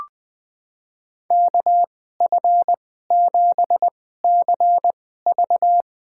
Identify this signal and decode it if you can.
Morse code, heard as a keyed tone, about 20 wpm, spaced as for 14 wpm, 700 Hz. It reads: KF7CV
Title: USA Call Signs - 2x Character Spacing - 20wpm